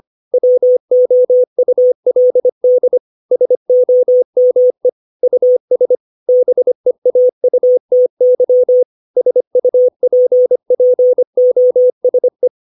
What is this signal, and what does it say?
Morse code, heard as a keyed tone, about 25 wpm, 500 Hz. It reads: WOULD SOME US BEAUTY SUPPOSE
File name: 632 words / 632 Words 25WPM